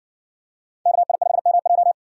Morse code, heard as a keyed tone, about 57 wpm, 700 Hz. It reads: KI5MJ